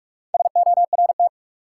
Morse code, heard as a keyed tone, about 44 wpm, 700 Hz. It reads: SORT
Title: Sentences from Top 1300 Words - New Word Drill Alt - 45wpm